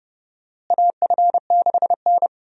Morse code, heard as a keyed tone, about 30 wpm, 700 Hz. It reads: AF6N
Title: USA Call Signs - 4 Characters - Repeated 2x - 30wpm